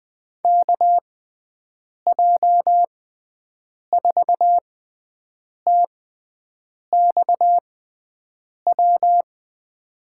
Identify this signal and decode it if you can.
Morse code, heard as a keyed tone, about 20 wpm, spaced as for 7 wpm, 700 Hz. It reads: KJ4TXW
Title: USA Call Signs - 6 Characters - 6x Character Spacing - 20wpm